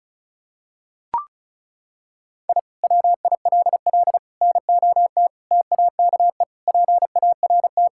Morse code, heard as a keyed tone, about 36 wpm, 700 Hz. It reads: I WILL NOT TAKE PART
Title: Sentences from Top 1000 Words - Review Part 6 - 35wpm